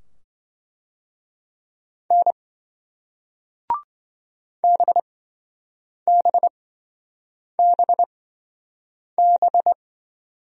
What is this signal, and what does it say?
Morse code, heard as a keyed tone, about 28 wpm, spaced as for 7 wpm, 700 Hz. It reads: N BBBB